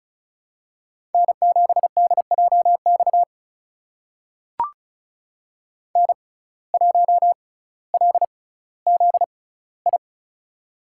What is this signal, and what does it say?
Morse code, heard as a keyed tone, about 36 wpm, spaced as for 18 wpm, 700 Hz. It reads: N7DJX N1LZI
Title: USA Call Signs - 5 Characters - 6x Character Spacing - 35wpm